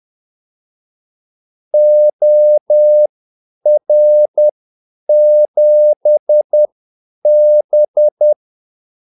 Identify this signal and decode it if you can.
Morse code, heard as a keyed tone, about 10 wpm, spaced as for 8 wpm, 600 Hz. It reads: OR7B
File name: Callsigns-10-8wpm